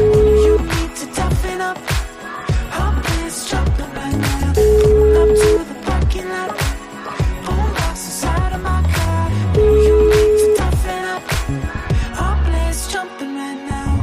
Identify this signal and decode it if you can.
Ringback tone (Europe, 425 Hz), with noise.